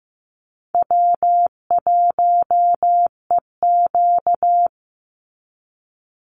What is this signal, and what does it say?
Morse code, heard as a keyed tone, about 15 wpm, 700 Hz. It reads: W1EQ
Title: Easy Call Signs - 15wpm